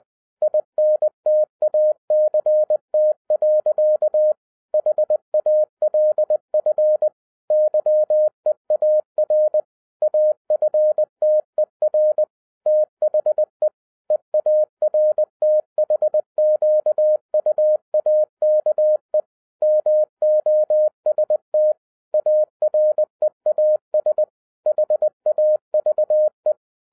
Morse code, heard as a keyed tone, about 20 wpm, 610 Hz. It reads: INTACT. HALF YEAR AFTER THE EARTHQUAKE MOST AREAS HAVE
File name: 2011-10-mp3 / n03-TARO JR0QWW 20wpm